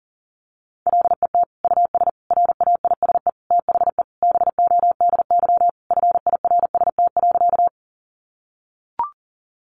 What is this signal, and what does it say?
Morse code, heard as a keyed tone, about 40 wpm, 700 Hz. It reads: LET US RAISE THE BODY FIRST.
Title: Sentences from Top 1400 Words 40wpm